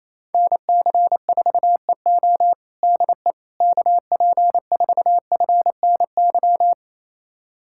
Morse code, heard as a keyed tone, about 28 wpm, 700 Hz. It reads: NC4EO DE KP4FNY
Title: USA Callsign DE Callsign 28wpm